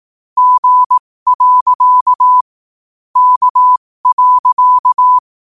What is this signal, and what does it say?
Morse code, heard as a keyed tone, about 18 wpm, spaced as for 15 wpm, 1000 Hz. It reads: G. K.